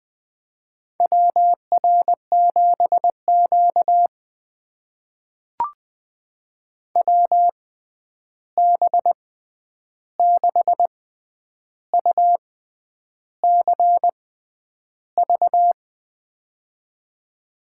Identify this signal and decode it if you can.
Morse code, heard as a keyed tone, about 20 wpm, spaced as for 9 wpm, 700 Hz. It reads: WR7Q WB6UCV
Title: USA Call Signs - 6x Character Spacing - 20wpm